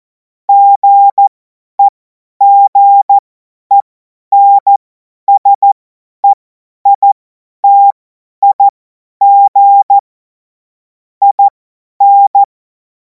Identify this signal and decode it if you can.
Morse code, heard as a keyed tone, about 14 wpm, spaced as for 10 wpm, 800 Hz. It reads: GEGENSEITIG IN
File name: Morsekurs / Witze / Witz 07. April 2026 50 F70